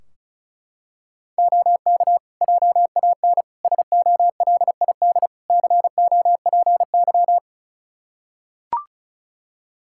Morse code, heard as a keyed tone, about 34 wpm, 700 Hz. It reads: OK JAN SOLID COPY
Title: K1USN SST QSO Practice 35wpm